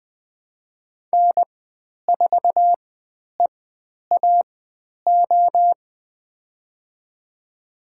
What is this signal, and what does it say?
Morse code, heard as a keyed tone, about 20 wpm, spaced as for 10 wpm, 700 Hz. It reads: N4EAO
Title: 095 Lesson 16g - Letter c - Call Signs 20wpm